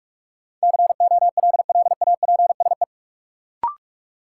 Morse code, heard as a keyed tone, about 43 wpm, 700 Hz. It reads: COLLAPSE
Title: Sets of 5 Words Encore - New Word Drill - 45wpm